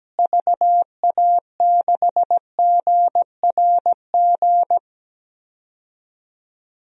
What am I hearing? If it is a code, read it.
Morse code, heard as a keyed tone, about 17 wpm, 700 Hz. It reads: VA6GRG
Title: DXCC-100 Call Signs - 6 Characters - Repeated 3x - 17wpm